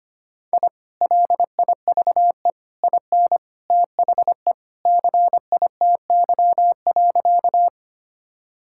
Morse code, heard as a keyed tone, about 25 wpm, 700 Hz. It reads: I LIVE IN THE CITY.
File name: practice / 25wpm / Sentences with 2-Word Phrases - 25wpm